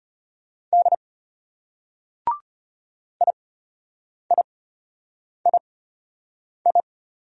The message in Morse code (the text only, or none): D IIII